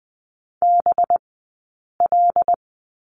Morse code, heard as a keyed tone, about 20 wpm, spaced as for 8 wpm, 700 Hz. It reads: BL
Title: 183 Lesson 30d - Letter b - Two Characters 20wpm